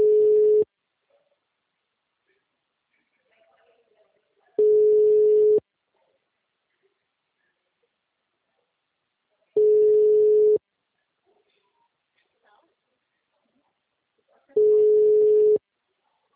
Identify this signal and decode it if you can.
Ringback tone (Europe, 425 Hz), with speech.
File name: out- -2006- - - .93117